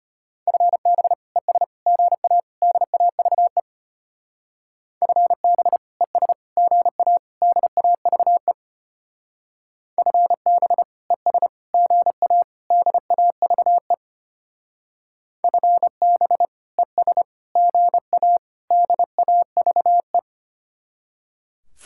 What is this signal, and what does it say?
Morse code, heard as a keyed tone, about 31 wpm, 700 Hz. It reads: FB ES GA DAVE FB ES GA DAVE FB ES GA DAVE FB ES GA DAVE
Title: Speed-Racing Common QSO Phrases - 25wpm